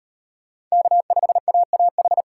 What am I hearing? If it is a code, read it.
Morse code, heard as a keyed tone, about 34 wpm, 700 Hz. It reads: K5AAH